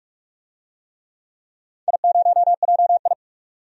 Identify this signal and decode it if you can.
Morse code, heard as a keyed tone, about 42 wpm, 700 Hz. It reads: I0JI
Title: DXCC-100 Call Signs - 4 Characters - 45wpm